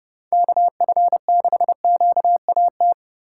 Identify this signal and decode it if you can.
Morse code, heard as a keyed tone, about 36 wpm, 700 Hz. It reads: KF6QAT
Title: Speed Racing - USA 2x3 Call Signs 30wpm